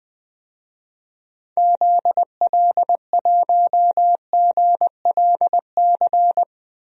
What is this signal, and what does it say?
Morse code, heard as a keyed tone, about 20 wpm, 700 Hz. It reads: ZL1GLC
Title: DXCC-100 Call Signs - 6 Characters - 20wpm